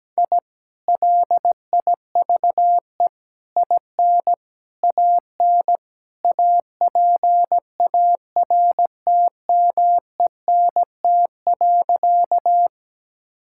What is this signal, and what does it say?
Morse code, heard as a keyed tone, about 17 wpm, 700 Hz. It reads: I LIVE IN AN APARTMENT.
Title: Sentences from Top 1000 Words - Review Part 6 - 17wpm